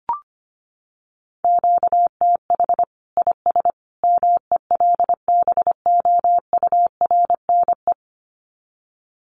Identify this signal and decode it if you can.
Morse code, heard as a keyed tone, about 25 wpm, 700 Hz. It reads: QTH IS MELBOURNE
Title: Common QSO Phrases 25wpm